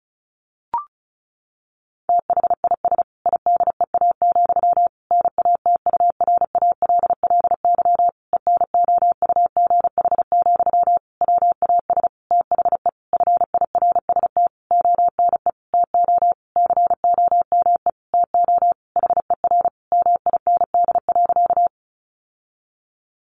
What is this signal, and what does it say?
Morse code, heard as a keyed tone, about 34 wpm, 700 Hz. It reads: THIS IDEA, NATURALLY ENOUGH, WAS THE FIRST ONE TO COME TO HER MIND.
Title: Sentences from Top 1300 Words 35wpm